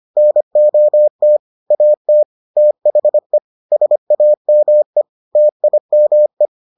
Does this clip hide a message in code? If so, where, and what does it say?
Yes: Morse code from 0.2 to 6.5 s, heard as a keyed tone, about 25 wpm, 600 Hz. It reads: NOT AT THE SAME TIME